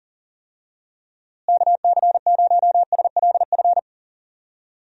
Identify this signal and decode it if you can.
Morse code, heard as a keyed tone, about 40 wpm, 700 Hz. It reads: KC0SLF
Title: USA Call Signs - 6 Characters - Repeated 3x - 40wpm